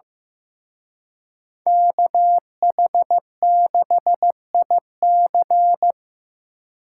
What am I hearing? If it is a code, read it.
Morse code, heard as a keyed tone, about 18 wpm, 700 Hz. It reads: KH6IC
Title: Speed Racing - USA 2x2 Call Signs 15wpm